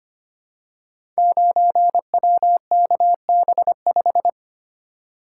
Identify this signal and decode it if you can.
Morse code, heard as a keyed tone, about 25 wpm, 700 Hz. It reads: 9WKB5
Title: Code Groups - 5 Letter:Numbers - 25wpm